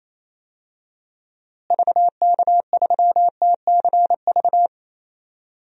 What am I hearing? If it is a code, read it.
Morse code, heard as a keyed tone, about 27 wpm, 700 Hz. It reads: VK3TCV
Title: DXCC-100 Call Signs - 6 Characters - Repeated 2x - 28wpm